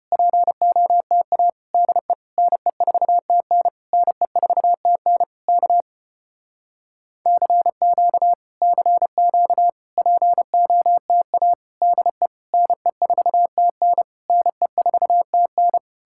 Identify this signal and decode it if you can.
Morse code, heard as a keyed tone, about 33 wpm, 700 Hz. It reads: POTA DE NE4TN NE4TN K CQ CQ POTA DE NE4TN NE4TN